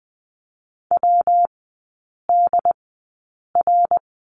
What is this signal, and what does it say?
Morse code, heard as a keyed tone, about 20 wpm, spaced as for 8 wpm, 700 Hz. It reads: WDR
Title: 114 Lesson 19e - Letter w - Three Characters 20wpm